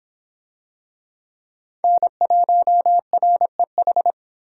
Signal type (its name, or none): Morse code